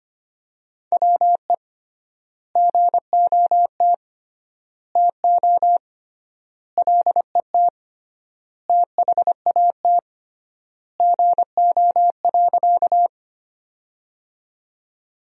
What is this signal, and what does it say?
Morse code, heard as a keyed tone, about 25 wpm, 700 Hz. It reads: WE GOT TO LET THAT GO.
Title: 3x Word-Spacing - Sentences from Top 300 Words - 25wpm